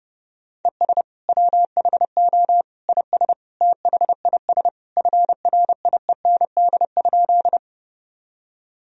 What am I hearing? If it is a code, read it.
Morse code, heard as a keyed tone, about 32 wpm, 700 Hz. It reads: ES WHO IS THIS FRIEND?
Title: Sentences from Top 300 Words and Abbreviations 30wpm